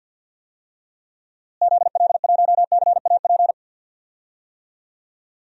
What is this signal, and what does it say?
Morse code, heard as a keyed tone, about 53 wpm, 700 Hz. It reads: ZL1CAP
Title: DXCC-100 Call Signs - 6 Characters - Repeated 2x - 50wpm